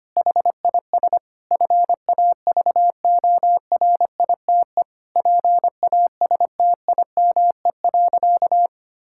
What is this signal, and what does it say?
Morse code, heard as a keyed tone, about 25 wpm, 700 Hz. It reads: HIS FAVORITE PASTIME.
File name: practice / 25wpm / Difficult Sentences with Hints - 25wpm